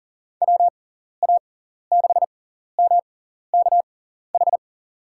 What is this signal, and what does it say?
Morse code, heard as a keyed tone, about 38 wpm, spaced as for 14 wpm, 700 Hz. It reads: WA6MKH
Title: USA Call Signs - 6 Characters - 6x Character Spacing - 40wpm